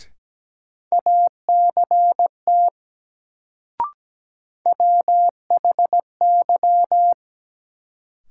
Morse code, heard as a keyed tone, about 17 wpm, 700 Hz. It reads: ACT WHY
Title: Sentences from Top 200 Words - New Word Drill - 17wpm